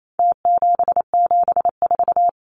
Morse code, heard as a keyed tone, about 28 wpm, 700 Hz. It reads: T774
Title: Numbers - Contest Style - 4-Digit 28wpm